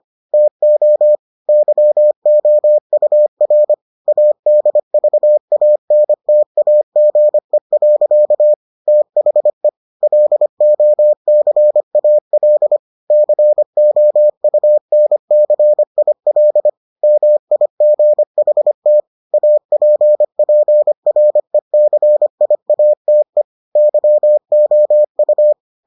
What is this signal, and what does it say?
Morse code, heard as a keyed tone, about 25 wpm, 600 Hz. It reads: TO YOUR ADVANTAGE. THE LOCAL COUNCIL MIGHT APPRECIATE YOU